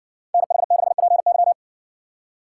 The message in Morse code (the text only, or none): NH6PY